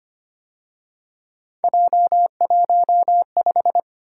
Morse code, heard as a keyed tone, about 29 wpm, 700 Hz. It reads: J15
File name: Speed-Racing - Three Letter-Number Combinations - 25wpm